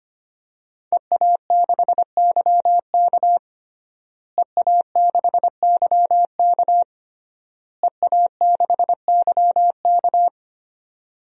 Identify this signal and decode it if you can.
Morse code, heard as a keyed tone, about 25 wpm, 700 Hz. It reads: EA6YK EA6YK EA6YK
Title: DXCC-100 Call Signs - 5 Characters - Repeated 3x - 25wpm